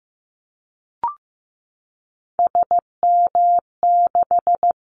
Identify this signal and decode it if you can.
Morse code, heard as a keyed tone, about 15 wpm, 700 Hz. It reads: SM6
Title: Callsign Prefixes for Top 25 DXCC - 15wpm